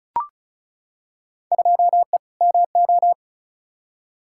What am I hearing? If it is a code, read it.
Morse code, heard as a keyed tone, about 36 wpm, 700 Hz. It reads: 2E MO
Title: Field Day Exchanges - 35wpm